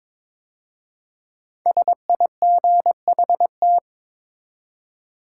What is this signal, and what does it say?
Morse code, heard as a keyed tone, about 22 wpm, 700 Hz. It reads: SIGHT
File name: 5 Letter Words - First 3 Letters Given - 22wpm